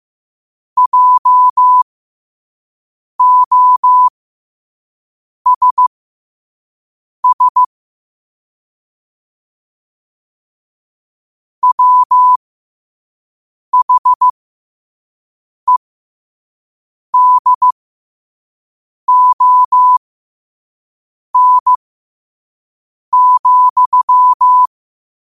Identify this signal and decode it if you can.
Morse code, heard as a keyed tone, about 15 wpm, spaced as for 5 wpm, 1000 Hz. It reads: JOSS WHEDON,
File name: qotdfiles / 5wpm